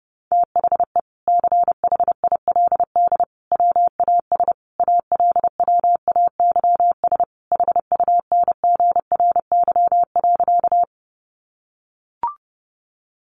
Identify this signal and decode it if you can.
Morse code, heard as a keyed tone, about 28 wpm, 700 Hz. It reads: THE CHILD WAS ALWAYS HUNGRY.
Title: Sentences from Top 1400 Words 30wpm